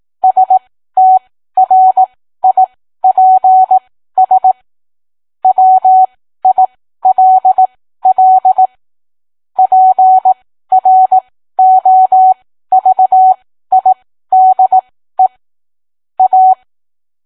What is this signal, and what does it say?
Morse code, heard as a keyed tone, about 18 wpm, spaced as for 13 wpm, 750 Hz. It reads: STRIPS WILL PROVIDE A